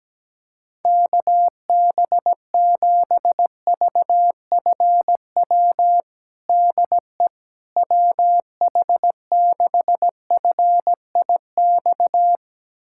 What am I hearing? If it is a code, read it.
Morse code, heard as a keyed tone, about 17 wpm, 700 Hz. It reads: KB7VFW DE WH6FIX
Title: USA Callsign DE Callsign - 6 Characters - 17wpm